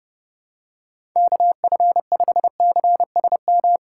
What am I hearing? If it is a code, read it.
Morse code, heard as a keyed tone, about 28 wpm, 700 Hz. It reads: KF5CSM